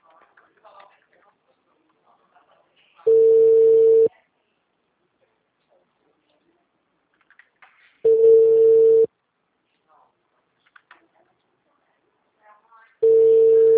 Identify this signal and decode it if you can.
Ringback tone (China, 450 Hz).
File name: out- -2003- - - .55